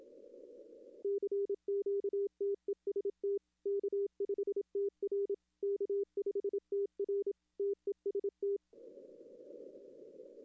Morse code, heard as a keyed tone, about 27 wpm, 400 Hz. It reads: CQ TEST K5TR K5TR TEST
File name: contests / arrl-160-k5tr-op-w5tn -0200